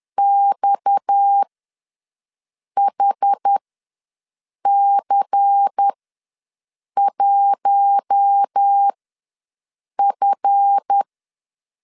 Morse code, heard as a keyed tone, about 10 wpm, spaced as for 6 wpm, 800 Hz. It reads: XHC1F
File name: LEZIONI / 7 GF9Z / 7 GF ws 10wpm 3918